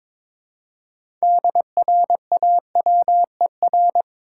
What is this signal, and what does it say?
Morse code, heard as a keyed tone, about 22 wpm, 700 Hz. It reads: DRAWER
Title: Six Letter Words - 22wpm